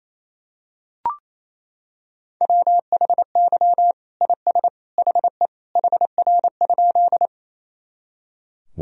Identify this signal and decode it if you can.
Morse code, heard as a keyed tone, about 28 wpm, 700 Hz. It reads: WHY IS HE HR?
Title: Sentences from Top 200 Words and Abbreviations 28wpm